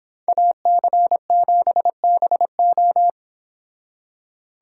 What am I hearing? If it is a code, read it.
Morse code, heard as a keyed tone, about 25 wpm, 700 Hz. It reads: AC7BO